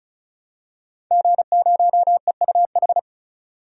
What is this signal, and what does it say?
Morse code, heard as a keyed tone, about 34 wpm, 700 Hz. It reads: G0EUH
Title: DXCC-100 Call Signs - 5 Characters - 35wpm